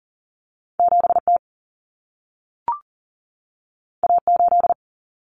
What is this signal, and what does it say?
Morse code, heard as a keyed tone, about 40 wpm, 700 Hz. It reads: 7T A8
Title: Common Cut Numbers - 2-Digit 40wpm